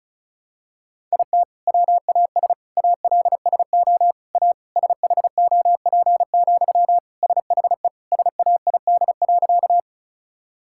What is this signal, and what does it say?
Morse code, heard as a keyed tone, about 36 wpm, 700 Hz. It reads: IT WAS ALSO A SHOP, SHE SAID.
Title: Sentences from Top 1600 Words 35wpm